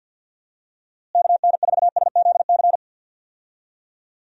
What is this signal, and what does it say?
Morse code, heard as a keyed tone, about 53 wpm, 700 Hz. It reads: KN4SZC
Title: USA Call Signs - 6 Characters - Repeated 2x - 50wpm